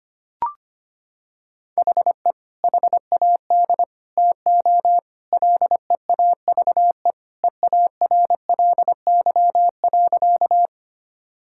Morse code, heard as a keyed tone, about 25 wpm, 700 Hz. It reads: HE HAD TO LEAVE EARLY.
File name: practice / 25wpm / Sentences with 2-Word Phrases - 25wpm